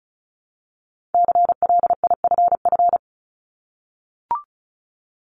Morse code, heard as a keyed tone, about 36 wpm, 700 Hz. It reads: CLIFF
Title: Multi-letter Phonemes Repeated 3x 35wpm